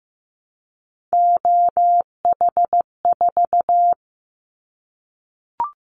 Morse code, heard as a keyed tone, about 15 wpm, 700 Hz. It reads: OH4
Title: Callsign Prefixes for Top 25 DXCC - 15wpm